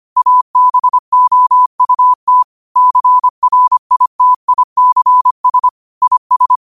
Morse code, heard as a keyed tone, about 25 wpm, 1000 Hz. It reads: ABOUT CRITICS IS
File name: qotdfiles / 25wpm